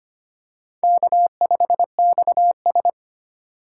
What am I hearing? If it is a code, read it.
Morse code, heard as a keyed tone, about 25 wpm, 700 Hz. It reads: K5XS
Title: USA Call Signs - 4 Characters - 4x Character Spacing - 25wpm